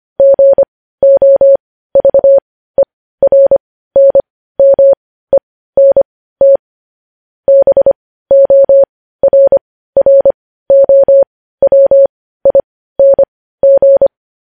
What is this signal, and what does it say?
Morse code, heard as a keyed tone, about 25 wpm, spaced as for 15 wpm, 550 Hz. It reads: GOVERNMENT BORROWING
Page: News Headlines in Morse Code at 15 WPM